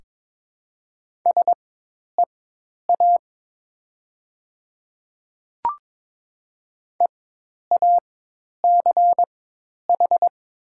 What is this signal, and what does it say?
Morse code, heard as a keyed tone, about 22 wpm, spaced as for 10 wpm, 700 Hz. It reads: SEA EACH
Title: 4x Character Spacing - Top 500 Words - 22wpm